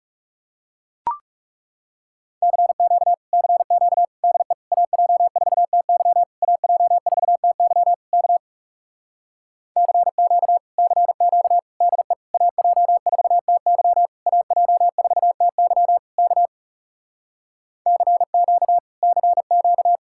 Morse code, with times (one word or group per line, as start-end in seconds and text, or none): 2.4-3.1 CQ
3.3-4.1 CQ
4.2-4.5 DE
4.7-6.2 AJ4TY
6.4-7.9 AJ4TY
8.1-8.4 K
9.8-10.6 CQ
10.8-11.6 CQ
11.8-12.1 DE
12.3-14.1 AJ4TY
14.3-16.0 AJ4TY
16.2-16.5 K
17.9-18.8 CQ
19.0-20.0 CQ